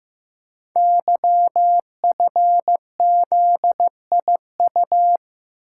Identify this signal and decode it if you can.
Morse code, heard as a keyed tone, about 15 wpm, 700 Hz. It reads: YFZIU
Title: Code Groups - 5 Letters - 15wpm